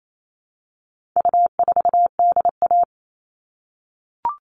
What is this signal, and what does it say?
Morse code, heard as a keyed tone, about 28 wpm, 700 Hz. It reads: U4DA
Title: All Cut Numbers - 4-Digit 28wpm